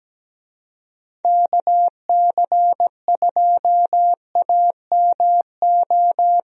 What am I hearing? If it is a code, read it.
Morse code, heard as a keyed tone, about 17 wpm, 700 Hz. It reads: KC2AMO
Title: USA Call Signs - 6 Characters - Repeated 2x - 17wpm